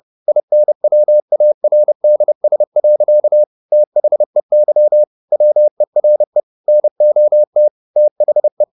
Morse code, heard as a keyed tone, about 30 wpm, 600 Hz. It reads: INWARDS. THEY WERE NOT THE